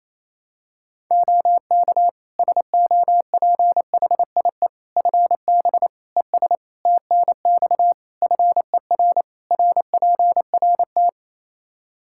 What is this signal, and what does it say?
Morse code, heard as a keyed tone, about 28 wpm, 700 Hz. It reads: OK SOPHIE FB ES TNX FER RPRT
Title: Practice QSO - Exchange 2 - Info - 28wpm